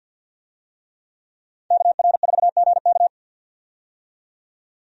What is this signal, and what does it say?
Morse code, heard as a keyed tone, about 49 wpm, 700 Hz. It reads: KR4GK
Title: USA Call Signs - 5 Characters - 50wpm